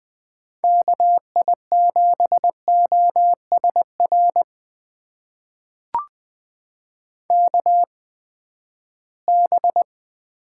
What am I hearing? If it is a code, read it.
Morse code, heard as a keyed tone, about 20 wpm, spaced as for 14 wpm, 700 Hz. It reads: KI7OSR KB
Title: USA Call Signs - 6 Characters - 8x Character Spacing - 20wpm